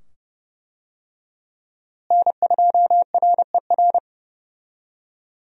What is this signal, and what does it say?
Morse code, heard as a keyed tone, about 32 wpm, 700 Hz. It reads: N2RER
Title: USA Call Signs - 5 Characters - Repeated 2x - 30wpm